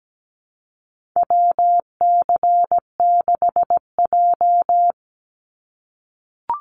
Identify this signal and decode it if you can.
Morse code, heard as a keyed tone, about 17 wpm, 700 Hz. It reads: WC6J
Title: NA SOTA Chaser Callsigns 17wpm